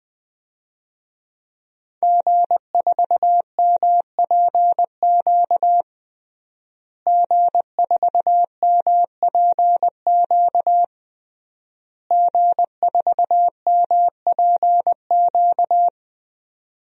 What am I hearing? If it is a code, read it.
Morse code, heard as a keyed tone, about 21 wpm, 700 Hz. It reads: G4MPQ G4MPQ G4MPQ
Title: DXCC-100 Call Signs - 5 Characters - Repeated 3x - 20wpm